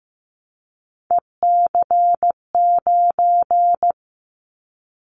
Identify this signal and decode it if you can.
Morse code, heard as a keyed tone, about 15 wpm, 700 Hz. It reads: EC9
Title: Callsign Prefixes for Top 100 DXCC - 15wpm